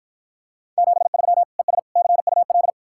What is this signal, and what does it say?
Morse code, heard as a keyed tone, about 54 wpm, 700 Hz. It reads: 73 ES CUL